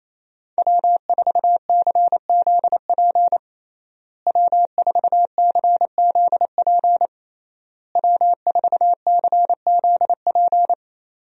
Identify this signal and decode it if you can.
Morse code, heard as a keyed tone, about 27 wpm, 700 Hz. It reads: W4CZP W4CZP W4CZP
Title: USA Call Signs - 5 Characters - Repeated 3x - 28wpm